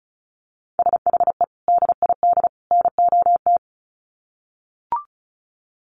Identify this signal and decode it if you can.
Morse code, heard as a keyed tone, about 34 wpm, 700 Hz. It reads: SHE DID NOT
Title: Sets of 3 Words 35wpm